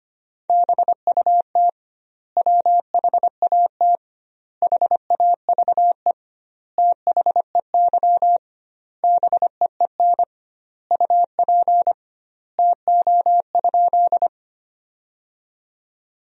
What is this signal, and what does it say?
Morse code, heard as a keyed tone, about 25 wpm, 700 Hz. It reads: BUT WHAT HAVE THEY BEEN UP TO?
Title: 2x Word-Spacing - Sentences from Top 100 Words - 25wpm